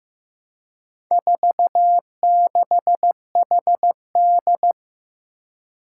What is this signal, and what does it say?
Morse code, heard as a keyed tone, about 15 wpm, 700 Hz. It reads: 46HD
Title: Four Challenging Letter-Number Combinations 15wpm